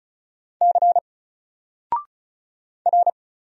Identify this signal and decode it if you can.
Morse code, heard as a keyed tone, about 34 wpm, 700 Hz. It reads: C R
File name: Single-Letter-Number-35wpm